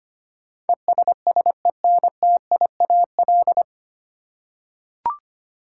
Morse code, heard as a keyed tone, about 25 wpm, 700 Hz. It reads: ESSENTIAL
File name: practice / 25wpm / 9 Letter Words - First 3 Letters Given - 25wpm